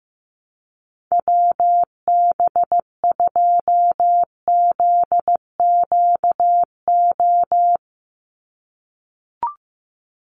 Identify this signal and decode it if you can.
Morse code, heard as a keyed tone, about 15 wpm, 700 Hz. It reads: WB2ZQO
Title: USA Call Signs 15wpm